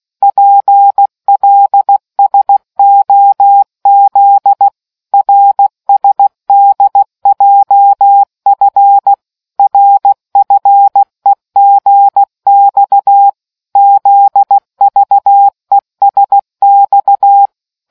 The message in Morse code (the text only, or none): PLSOZ RSDJF RFEGX ZVESX